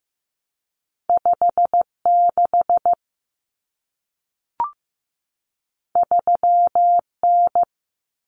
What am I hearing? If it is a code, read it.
Morse code, heard as a keyed tone, about 15 wpm, 700 Hz. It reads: 56 3N
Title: Common Cut Numbers - 2-Digit 15wpm